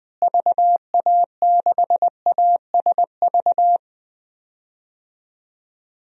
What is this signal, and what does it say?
Morse code, heard as a keyed tone, about 20 wpm, 700 Hz. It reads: VA6ASV
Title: DXCC-100 Call Signs - 6 Characters - Repeated 2x - 20wpm